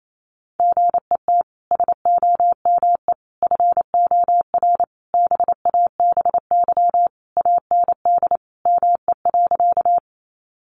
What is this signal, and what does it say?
Morse code, heard as a keyed tone, about 28 wpm, 700 Hz. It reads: GET SOME FOR BABY AND ME.
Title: Sentences from Top 1300 Words 28wpm